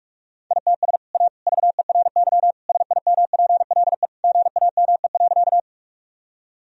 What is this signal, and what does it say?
Morse code, heard as a keyed tone, about 46 wpm, 700 Hz. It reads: ITS A VERY SIMPLE GAME.
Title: Sentences from Top 1000 Words - Review Part 3 - 45wpm